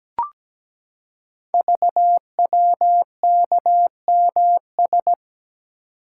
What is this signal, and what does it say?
Morse code, heard as a keyed tone, about 17 wpm, 700 Hz. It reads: VWKMS
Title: Code Groups - 5 Letters - 17wpm